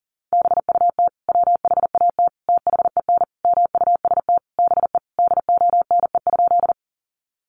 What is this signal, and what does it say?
Morse code, heard as a keyed tone, about 40 wpm, 700 Hz. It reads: BUT WHAT THEN MUST BE DONE?
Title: Sentences from Top 500 Words 40wpm